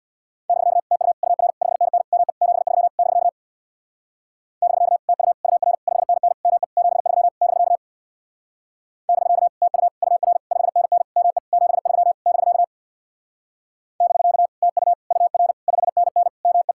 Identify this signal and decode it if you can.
Morse code, heard as a keyed tone, about 59 wpm, 700 Hz. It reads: <BK> TU UR 5NN ME 73 <BK> <BK> TU UR 5NN ME 73 <BK> <BK> TU UR 5NN ME 73 <BK> <BK> TU UR 5NN ME
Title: Speed-Racing Common QSO Phrases - 50wpm